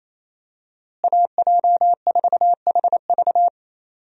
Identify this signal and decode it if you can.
Morse code, heard as a keyed tone, about 28 wpm, 700 Hz. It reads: AJ4HV
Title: USA Call Signs - 5 Characters - Repeated 3x - 28wpm